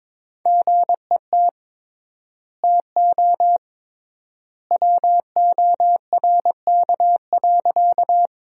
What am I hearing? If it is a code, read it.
Morse code, heard as a keyed tone, about 22 wpm, 700 Hz. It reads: GET TO WORK.
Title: 3x Word-Spacing - Sentences from Top 200 Words - 22wpm